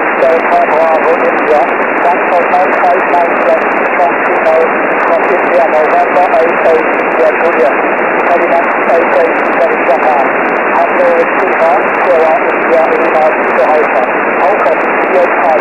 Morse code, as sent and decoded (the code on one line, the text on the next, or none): none